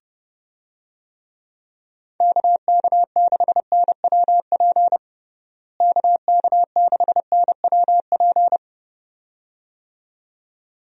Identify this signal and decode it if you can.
Morse code, heard as a keyed tone, about 30 wpm, 700 Hz. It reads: KK6NWP KK6NWP
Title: USA Call Signs - 6 Characters - Repeated 2x - 30wpm